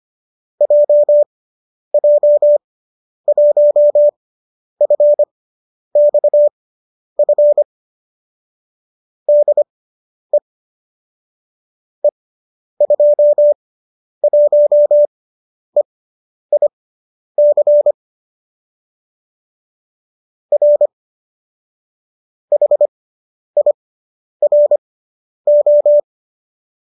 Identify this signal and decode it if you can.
Morse code, heard as a keyed tone, about 25 wpm, spaced as for 10 wpm, 570 Hz. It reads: JJ1FXF DE E21EIC R HIRO
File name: qso005 10